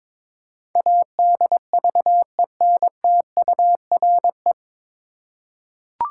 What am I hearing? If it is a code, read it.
Morse code, heard as a keyed tone, about 22 wpm, 700 Hz. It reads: ADVENTURE
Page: Morse Code Ninja practice 22wpm